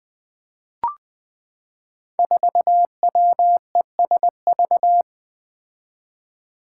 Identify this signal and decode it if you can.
Morse code, heard as a keyed tone, about 20 wpm, 700 Hz. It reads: 4WESV
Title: Five Challenging Letter-Number Combinations 20wpm